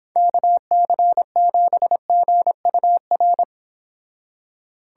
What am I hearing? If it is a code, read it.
Morse code, heard as a keyed tone, about 29 wpm, 700 Hz. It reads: KC7GUR